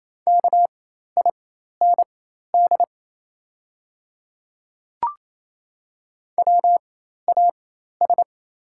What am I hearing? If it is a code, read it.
Morse code, heard as a keyed tone, about 27 wpm, spaced as for 13 wpm, 700 Hz. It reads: KIND WAS
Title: 4x Character Spacing - Top 300 Words - 28wpm